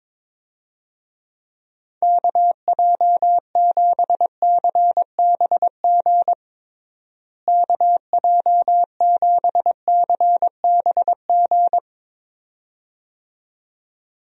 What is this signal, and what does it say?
Morse code, heard as a keyed tone, about 22 wpm, 700 Hz. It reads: KJ7CBG KJ7CBG
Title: USA Call Signs - 6 Characters - Repeated 2x - 22wpm